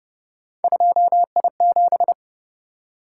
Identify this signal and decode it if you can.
Morse code, heard as a keyed tone, about 30 wpm, 700 Hz. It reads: 2I7